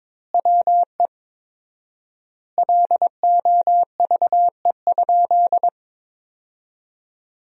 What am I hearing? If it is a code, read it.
Morse code, heard as a keyed tone, about 22 wpm, 700 Hz. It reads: WE LOVE?
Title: 4x Word-Spacing - Sentences from Top 400 Words - 22wpm